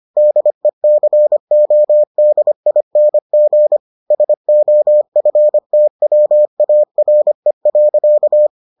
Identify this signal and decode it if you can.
Morse code, heard as a keyed tone, about 25 wpm, 600 Hz. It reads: DECODING SOFTWARE.